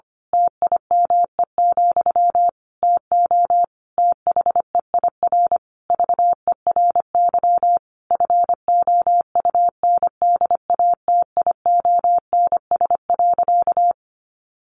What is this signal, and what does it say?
Morse code, heard as a keyed tone, about 25 wpm, 700 Hz. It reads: TIME, TO THEIR VERY FOUNDATIONS.
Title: A Sound Of Thunder 25wpm